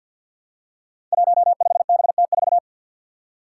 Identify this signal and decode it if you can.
Morse code, heard as a keyed tone, about 53 wpm, 700 Hz. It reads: 15BT4
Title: Five Challenging Letter-Number Combinations 50wpm